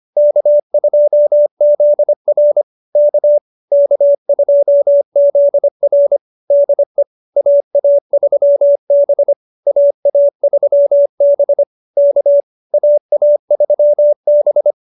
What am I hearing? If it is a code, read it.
Morse code, heard as a keyed tone, about 25 wpm, 590 Hz. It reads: K2ZR K K2ZR DE AA3B AA3B K AA3B